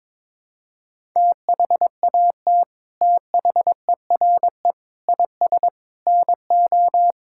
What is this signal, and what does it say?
Morse code, heard as a keyed tone, about 22 wpm, 700 Hz. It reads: THAT THERE IS NO
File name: practice / 22wpm / Sets of 8 Words - 22wpm